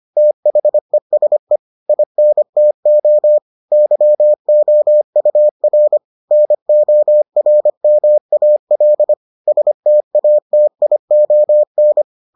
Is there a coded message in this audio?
Yes: Morse code from 0.2 to 12.0 s, heard as a keyed tone, about 25 wpm, 600 Hz. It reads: THESE INTO YOUR NORMAL STATION